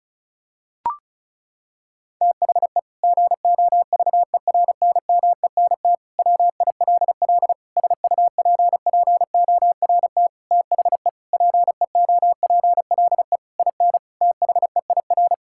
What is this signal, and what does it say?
Morse code, heard as a keyed tone, about 36 wpm, 700 Hz. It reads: THE GOVERNMENT WILL SUPPORT THE PEOPLE IN THEIR
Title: Sentences from Top 1000 Words - Review Part 5 - 35wpm